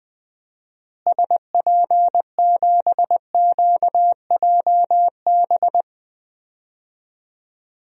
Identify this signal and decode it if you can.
Morse code, heard as a keyed tone, about 20 wpm, 700 Hz. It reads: SP7QJB
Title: DXCC-100 Call Signs - 6 Characters - Repeated 3x - 20wpm